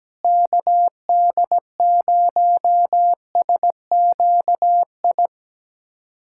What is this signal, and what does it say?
Morse code, heard as a keyed tone, about 17 wpm, 700 Hz. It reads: KD0SQI
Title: USA Callsign DE Callsign 17wpm